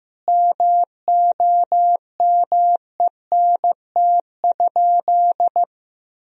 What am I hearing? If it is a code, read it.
Morse code, heard as a keyed tone, about 15 wpm, 700 Hz. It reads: MOMENT?
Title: Sentences from Top 1000 Words - Review Part 1 - 15wpm